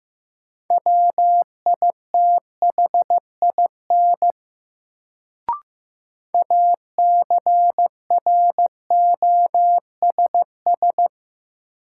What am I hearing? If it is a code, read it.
Morse code, heard as a keyed tone, about 15 wpm, 700 Hz. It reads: WITHIN ACROSS
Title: Sentences from Top 700 Words - New Word Drill - 15wpm